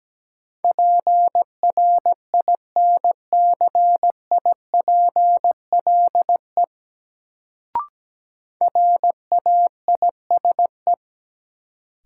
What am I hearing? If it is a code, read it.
Morse code, heard as a keyed tone, about 17 wpm, 700 Hz. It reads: PRINCIPLE RAISE
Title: Sentences from Top 1400 Words - New Word Drill Alt - 17wpm